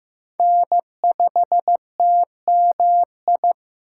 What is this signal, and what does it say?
Morse code, heard as a keyed tone, about 18 wpm, 700 Hz. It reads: N5TMI
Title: Speed Racing - USA Call Signs 15wpm